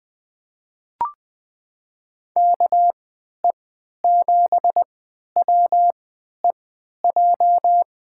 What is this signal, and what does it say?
Morse code, heard as a keyed tone, about 21 wpm, spaced as for 11 wpm, 700 Hz. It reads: KE7WEJ
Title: USA Call Signs - 6 Characters - 3x Character Spacing - 20wpm